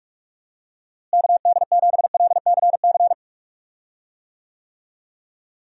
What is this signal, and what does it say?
Morse code, heard as a keyed tone, about 46 wpm, 700 Hz. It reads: KD7LCC
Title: USA Call Signs - 6 Characters - Repeated 2x - 45wpm